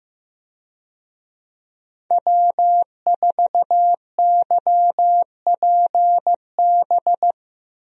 Morse code, heard as a keyed tone, about 16 wpm, 700 Hz. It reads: W4YPB
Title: USA Call Signs - 5 Characters - Repeated 2x - 15wpm